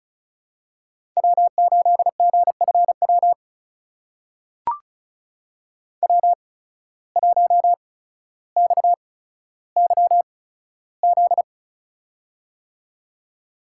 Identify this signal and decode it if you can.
Morse code, heard as a keyed tone, about 36 wpm, spaced as for 15 wpm, 700 Hz. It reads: W8GFW W1XYZ
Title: USA Call Signs - 5 Characters - 8x Character Spacing - 35wpm